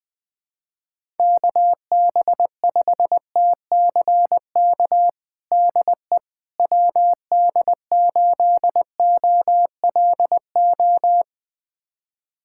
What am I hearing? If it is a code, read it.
Morse code, heard as a keyed tone, about 20 wpm, 700 Hz. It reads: KB5TCK DE WD8OLO
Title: USA Callsign DE Callsign - 6 Characters - 20wpm